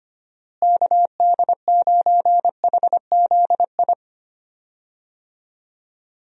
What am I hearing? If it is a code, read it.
Morse code, heard as a keyed tone, about 25 wpm, 700 Hz. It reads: KD9HZI
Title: USA Call Signs - 6 Characters - Repeated 2x - 25wpm